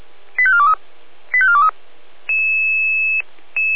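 Five-tone selective call (ZVEI), heard as quick runs of five tones, tones of about 70 ms. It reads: 85423 85423
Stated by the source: Five-tone ZVEI selective call received on an SDR, from audio.f85360.feuerwehr-orlamuende.org, the Orlamünde fire brigade.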